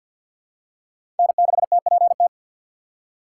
Morse code, heard as a keyed tone, about 53 wpm, 700 Hz. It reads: N6TPT
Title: USA Call Signs - 5 Characters - 2x Character Spacing - 50wpm